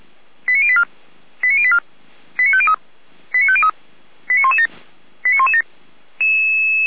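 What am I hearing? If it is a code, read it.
Five-tone selective call (ZVEI), heard as quick runs of five tones, tones of about 70 ms. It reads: 89084 89084 89503 89503 89117 89117
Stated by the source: Five-tone ZVEI selective call received on SDR, from feuerwehr-orlamuende.org (Feuerwehr Orlamünde fire service).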